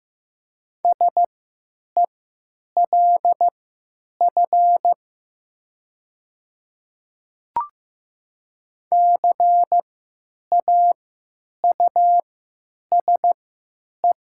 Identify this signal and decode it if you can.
Morse code, heard as a keyed tone, about 15 wpm, spaced as for 9 wpm, 700 Hz. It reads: SELF CAUSE
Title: 3x Character Spacing - Top 300 Words - 15wpm